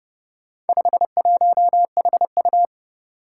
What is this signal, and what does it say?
Morse code, heard as a keyed tone, about 30 wpm, 700 Hz. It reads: 51HU
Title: Four Challenging Letter-Number Combinations 30wpm